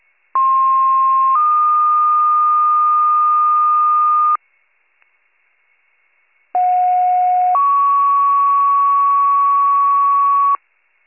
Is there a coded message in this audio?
Yes: two-tone page from 0.3 to 10.6 s, heard as a one-second tone, then a three-second tone, about 1060 then 1220 Hz, about 730 then 1120 Hz.